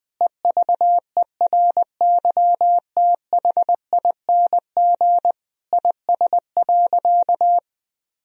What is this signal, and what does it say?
Morse code, heard as a keyed tone, about 20 wpm, 700 Hz. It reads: EVERYTHING IS.
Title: Sentences from Top 1000 Words - Review Part 4 - 20wpm